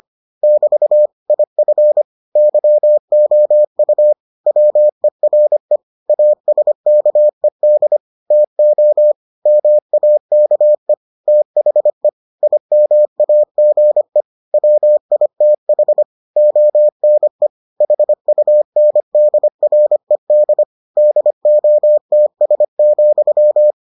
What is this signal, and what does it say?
Morse code, heard as a keyed tone, about 25 wpm, 600 Hz. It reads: <BT> IF YOU WERE ASKED TO MAKE THE IMAGE WITH ONE HUNDRED DOTS,